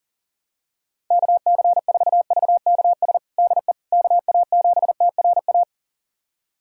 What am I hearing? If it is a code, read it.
Morse code, heard as a keyed tone, about 40 wpm, 700 Hz. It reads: KC4VKS DE KA7TRA
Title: USA Callsign DE Callsign 40wpm